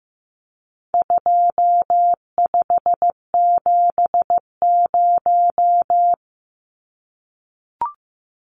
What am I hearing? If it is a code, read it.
Morse code, heard as a keyed tone, about 14 wpm, 700 Hz. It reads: 2570
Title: Numbers - 4-Digit 15wpm